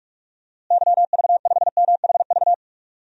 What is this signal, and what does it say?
Morse code, heard as a keyed tone, about 46 wpm, 700 Hz. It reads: YV5MHV
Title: DXCC-100 Call Signs - 6 Characters - Repeated 3x - 45wpm